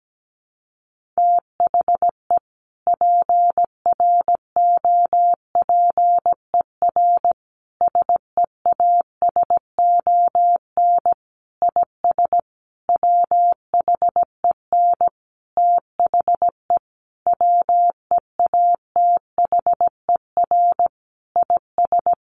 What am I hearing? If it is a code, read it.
Morse code, heard as a keyed tone, about 17 wpm, 700 Hz. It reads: THE PROPER SEASON IS WHEN THE WEATHER IS
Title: Sentences from Top 1400 Words 17wpm